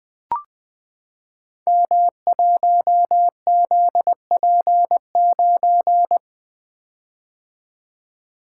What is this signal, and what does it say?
Morse code, heard as a keyed tone, about 20 wpm, 700 Hz. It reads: M1ZP9
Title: Code Groups - 5 Letter:Numbers - 20wpm